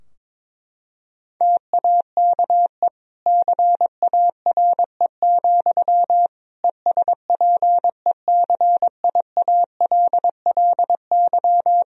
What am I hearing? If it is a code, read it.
Morse code, heard as a keyed tone, about 22 wpm, 700 Hz. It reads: TAKE CARE, ESPECIALLY
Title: Sentences from Top 1000 Words - Review Part 4 - 22wpm